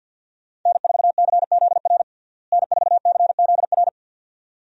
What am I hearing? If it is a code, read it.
Morse code, heard as a keyed tone, about 52 wpm, 700 Hz. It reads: N4CZR N4CZR
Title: USA Call Signs - 5 Characters - Repeated 2x - 50wpm